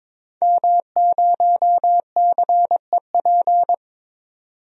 Morse code, heard as a keyed tone, about 22 wpm, 700 Hz. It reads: M0CEP
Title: DXCC-100 Call Signs - 5 Characters - 22wpm